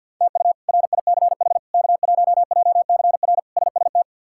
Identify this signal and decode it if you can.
Morse code, heard as a keyed tone, about 50 wpm, 700 Hz. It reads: TU RICH K1JCR SST
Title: K1USN SST QSO Practice - 50wpm